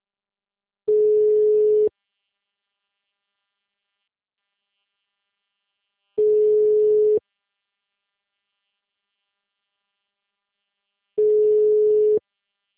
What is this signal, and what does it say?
Ringback tone (Europe, 425 Hz).
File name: out- -2034- - - .80317